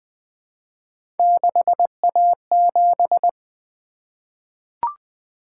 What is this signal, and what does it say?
Morse code, heard as a keyed tone, about 20 wpm, 700 Hz. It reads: 6A7
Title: Three Letter-Number Combinations - 20wpm